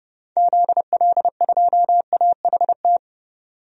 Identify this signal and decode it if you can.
Morse code, heard as a keyed tone, about 30 wpm, 700 Hz. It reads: ZL2AHT
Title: DXCC-100 Call Signs - 6 Characters - 30wpm